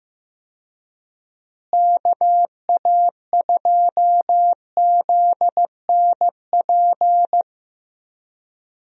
Morse code, heard as a keyed tone, about 15 wpm, 700 Hz. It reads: KA2ZNP
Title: USA Call Signs - 6 Characters - Repeated 2x - 15wpm